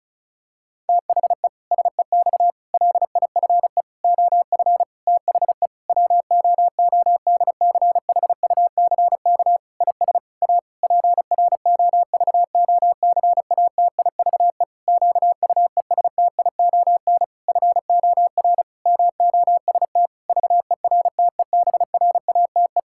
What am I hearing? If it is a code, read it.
Morse code, heard as a keyed tone, about 36 wpm, 700 Hz. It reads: THE SEX LIFE OF THE WOODCHUCK IS A PROVOCATIVE QUESTION FOR MOST VERTEBRATE
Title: Pangram Challenge v2 - 35wpm